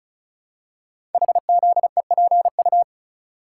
Morse code, heard as a keyed tone, about 36 wpm, 700 Hz. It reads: HZEPU
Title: Code Groups - 5 Letters - 35wpm